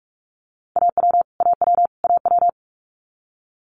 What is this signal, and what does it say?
Morse code, heard as a keyed tone, about 46 wpm, 700 Hz. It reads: AW AW AW
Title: Multi-letter Phonemes Repeated 3x 45wpm